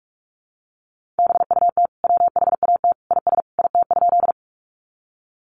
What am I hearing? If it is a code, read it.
Morse code, heard as a keyed tone, about 44 wpm, 700 Hz. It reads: BUT WHAT IS IT?
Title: Sentences from Top 100 Words 45wpm